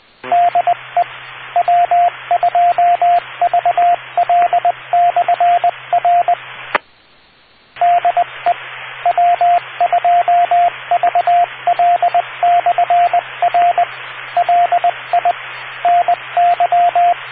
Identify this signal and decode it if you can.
Morse code, heard as a keyed tone, about 20 wpm, 700 Hz, with radio noise. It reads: DE W2VL/R DE W2VL/R LI NY